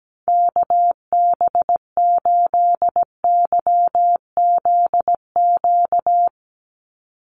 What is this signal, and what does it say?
Morse code, heard as a keyed tone, about 17 wpm, 700 Hz. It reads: KB8YZQ
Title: USA Call Signs 17wpm